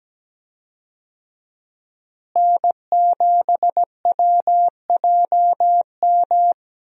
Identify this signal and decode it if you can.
Morse code, heard as a keyed tone, about 17 wpm, 700 Hz. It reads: N7WJM
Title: USA Call Signs - 5 Characters - Repeated 2x - 17wpm